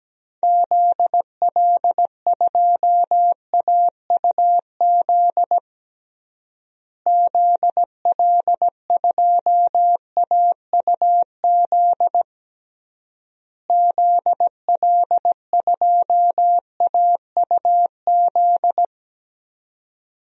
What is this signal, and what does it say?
Morse code, heard as a keyed tone, about 17 wpm, 700 Hz. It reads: ZL2AUZ ZL2AUZ ZL2AUZ
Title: DXCC-100 Call Signs - 6 Characters - Repeated 3x - 17wpm